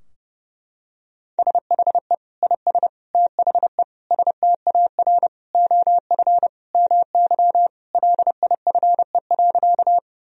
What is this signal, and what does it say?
Morse code, heard as a keyed tone, about 30 wpm, 700 Hz. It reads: SHE IS THE STAR OF MY LIFE.
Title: Sentences from Top 1000 Words - Review Part 2 - 30wpm